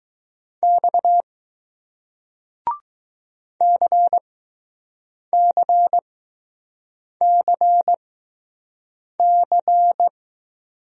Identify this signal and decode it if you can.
Morse code, heard as a keyed tone, about 20 wpm, spaced as for 6 wpm, 700 Hz. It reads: X CCCC